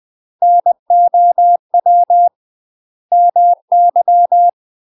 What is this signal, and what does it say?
Morse code, heard as a keyed tone, about 20 wpm, 700 Hz. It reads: NOW MY
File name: UK MASTER QSOs TRAINING / 20WPM / UK MASTER QSOs 0028